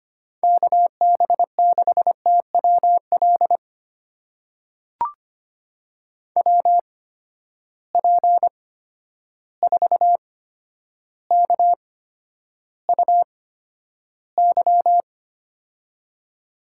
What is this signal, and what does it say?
Morse code, heard as a keyed tone, about 25 wpm, spaced as for 11 wpm, 700 Hz. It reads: KB6TWL WP4KUY
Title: USA Call Signs - 6 Characters - 8x Character Spacing - 25wpm